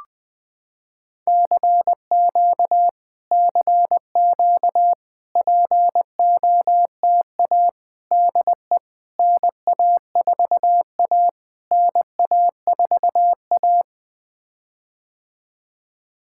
Morse code, heard as a keyed tone, about 21 wpm, 700 Hz. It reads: CQ CQ POTA DE NA4A NA4A
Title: POTA QSO Practice - 20wpm